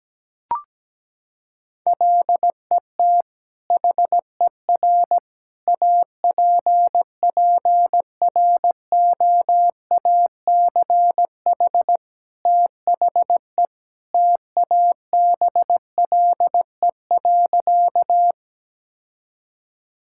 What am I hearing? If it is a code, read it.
Morse code, heard as a keyed tone, about 17 wpm, 700 Hz. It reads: LET HER APPROACH THE TABLE.
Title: Sentences from Top 1000 Words - Review Part 6 - 17wpm